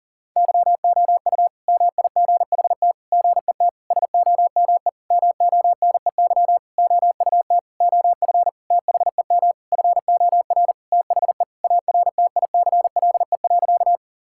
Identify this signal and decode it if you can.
Morse code, heard as a keyed tone, about 40 wpm, 700 Hz. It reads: YOU MIGHT GET SOME MONEY OUT OF THEM FOR THE ARTICLE.
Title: Sentences from Top 1000 Words - Review Part 5 - 40wpm